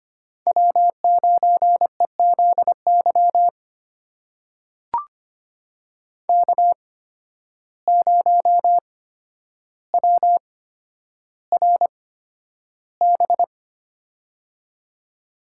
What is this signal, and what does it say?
Morse code, heard as a keyed tone, about 25 wpm, spaced as for 11 wpm, 700 Hz. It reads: W9EZY K0WRB
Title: USA Call Signs - 5 Characters - 8x Character Spacing - 25wpm